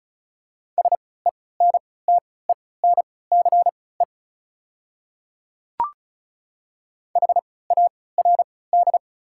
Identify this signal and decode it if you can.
Morse code, heard as a keyed tone, about 34 wpm, spaced as for 20 wpm, 700 Hz. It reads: SENTENCE HARD
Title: 3x Character Spacing - Top 300 Words - 35wpm